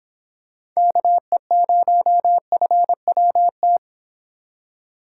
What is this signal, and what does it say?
Morse code, heard as a keyed tone, about 25 wpm, 700 Hz. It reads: KE0FWT